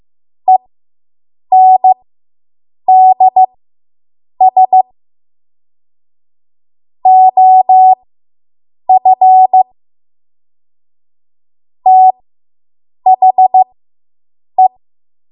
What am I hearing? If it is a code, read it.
Morse code, heard as a keyed tone, about 15 wpm, spaced as for 7 wpm, 750 Hz. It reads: ENDS OF THE